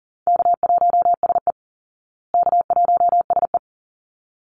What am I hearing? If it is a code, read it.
Morse code, heard as a keyed tone, about 40 wpm, 700 Hz. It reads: K1SE K1SE
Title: Easy Call Signs Repeated 2x - 40wpm